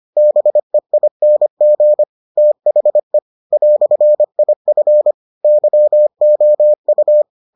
Morse code, heard as a keyed tone, about 25 wpm, 600 Hz. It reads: BEING THE "IF YOU